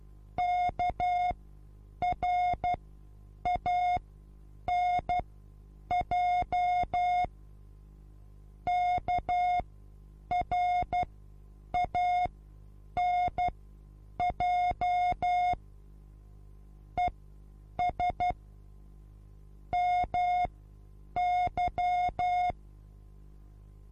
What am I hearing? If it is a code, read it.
Morse code, heard as a keyed tone, about 12 wpm, spaced as for 8 wpm, 720 Hz. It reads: KRANJ KRANJ ES MY
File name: Onlinelehrgang / morsen / DJ4UF Morsen23